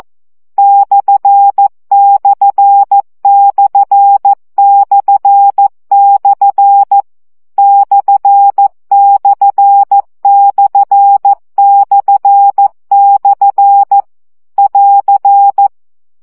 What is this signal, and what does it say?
Morse code, heard as a keyed tone, about 14 wpm, 800 Hz. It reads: ///// ///// <AR>
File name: CW gyakorlas per 12 baud